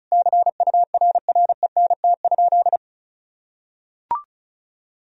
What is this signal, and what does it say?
Morse code, heard as a keyed tone, about 36 wpm, 700 Hz. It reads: CURRENT?
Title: Sentences from Top 1000 Words - Review Part 4 - 35wpm